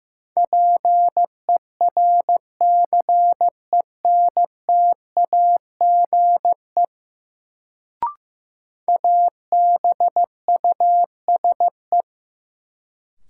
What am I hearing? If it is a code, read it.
Morse code, heard as a keyed tone, about 15 wpm, 700 Hz. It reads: PERCENTAGE ABUSE
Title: Sentences from Top 1300 Words - New Word Drill - 15wpm